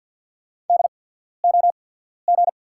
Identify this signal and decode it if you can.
Morse code, heard as a keyed tone, about 48 wpm, spaced as for 14 wpm, 700 Hz. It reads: DOG
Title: 8x Character Spacing - Top 500 Words - 50wpm